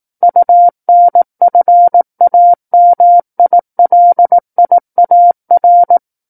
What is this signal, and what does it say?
Morse code, heard as a keyed tone, about 18 wpm, 700 Hz. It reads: UNFAMILIAR
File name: 18 WPM